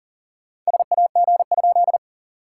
Morse code, heard as a keyed tone, about 40 wpm, 700 Hz. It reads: SAG?
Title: SOTA QSO Practice 40wpm